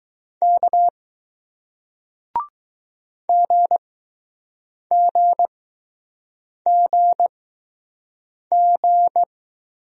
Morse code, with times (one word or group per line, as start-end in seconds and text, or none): 0.4-0.9 K
3.3-9.2 GGGG